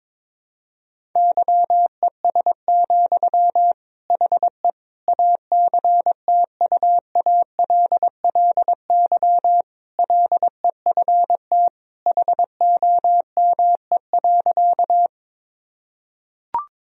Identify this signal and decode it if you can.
Morse code, heard as a keyed tone, about 22 wpm, 700 Hz. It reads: YES, HE ACTUALLY LEFT HOME.
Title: Sentences from Top 1000 Words - Review Part 4 - 22wpm